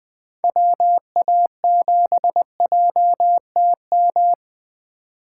Morse code, heard as a keyed tone, about 20 wpm, 700 Hz. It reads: WA7JTM
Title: 2023 SOTA Activator Call Signs - 20wpm